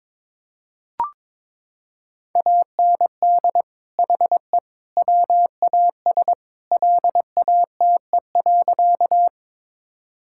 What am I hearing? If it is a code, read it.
Morse code, heard as a keyed tone, about 22 wpm, 700 Hz. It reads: AND HE WAS LATE.
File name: practice / 22wpm / Sentences with 3-Word Phrases - 22wpm